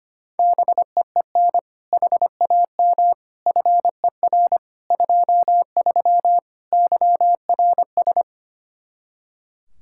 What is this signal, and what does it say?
Morse code, heard as a keyed tone, about 25 wpm, 700 Hz. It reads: BEEN HAM FER 23 YRS
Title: Practice QSO - Exchange 3 - More Info - 25wpm